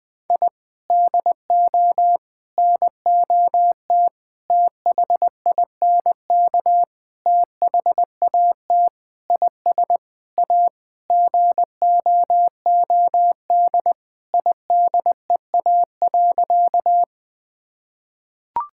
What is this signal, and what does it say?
Morse code, heard as a keyed tone, about 20 wpm, 700 Hz. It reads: I DO NOT THINK THAT IS A GOOD IDEA.
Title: Sentences with 5-Word Phrases - 20wpm